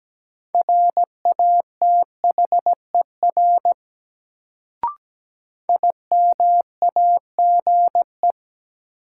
Morse code, heard as a keyed tone, about 17 wpm, 700 Hz. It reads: RATHER IMAGE
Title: Sentences from Top 700 Words - New Word Drill - 17wpm